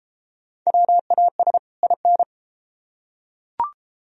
Morse code, heard as a keyed tone, about 33 wpm, 700 Hz. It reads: WAS IN